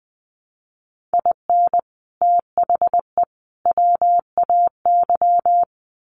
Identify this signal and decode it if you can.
Morse code, heard as a keyed tone, about 20 wpm, 700 Hz. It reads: IN THE WAY
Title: Sets of 3 Words 20wpm